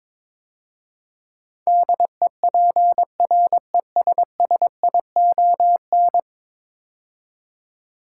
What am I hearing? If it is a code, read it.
Morse code, heard as a keyed tone, about 22 wpm, 700 Hz. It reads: DEPRESSION
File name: practice / 22wpm / 10 Letter Words - First 3 Letters Given - 22wpm